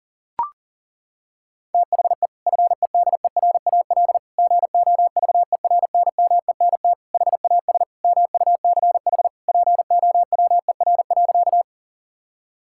Morse code, heard as a keyed tone, about 40 wpm, 690 Hz. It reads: THE FEDERAL GOVERNMENT HAS MUCH POWER.
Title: Sentences from Top 1000 Words - Review Part 5 - 40wpm